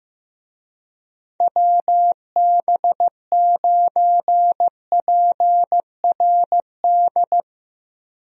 Morse code, heard as a keyed tone, about 15 wpm, 700 Hz. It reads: WB9PRD
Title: USA Call Signs - 6 Characters - Repeated 2x - 15wpm